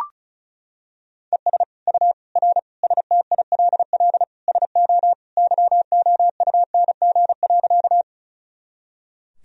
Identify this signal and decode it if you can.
Morse code, heard as a keyed tone, about 34 wpm, 700 Hz. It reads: ES U R STILL SO YOUNG.
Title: Sentences from Top 400 Words and Abbreviations 35wpm